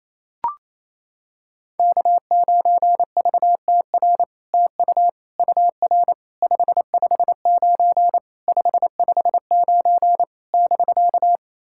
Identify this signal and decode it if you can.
Morse code, heard as a keyed tone, about 29 wpm, 700 Hz. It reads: K9VTR TU UR 559 559 <BK>
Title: SOTA QSO Practice 28wpm